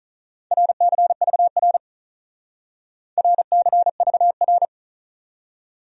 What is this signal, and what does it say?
Morse code, heard as a keyed tone, about 46 wpm, 700 Hz. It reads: RCVR RCVR